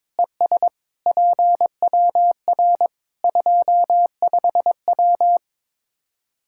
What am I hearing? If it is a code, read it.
Morse code, heard as a keyed tone, about 22 wpm, 700 Hz. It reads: ES PWR 25W
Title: Practice QSO - Exchange 2 - Info - 22wpm